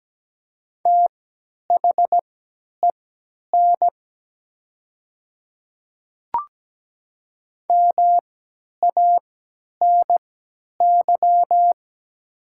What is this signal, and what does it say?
Morse code, heard as a keyed tone, about 17 wpm, spaced as for 10 wpm, 700 Hz. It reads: THEN MANY
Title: 3x Character Spacing - Top 100 Words - 17wpm